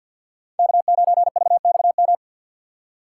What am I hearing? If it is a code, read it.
Morse code, heard as a keyed tone, about 52 wpm, 700 Hz. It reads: K9VXM
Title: USA Call Signs - 5 Characters - 4x Character Spacing - 50wpm